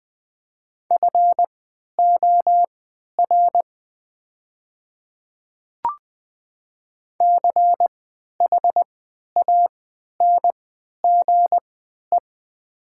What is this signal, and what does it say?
Morse code, heard as a keyed tone, about 20 wpm, spaced as for 11 wpm, 700 Hz. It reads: FOR CHANGE
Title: 3x Character Spacing - Top 200 Words - 20wpm